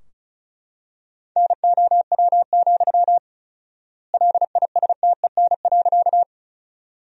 Morse code, heard as a keyed tone, about 34 wpm, 700 Hz. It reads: NOW, LISTEN.
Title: 4x Word-Spacing - Sentences from Top 400 Words - 35wpm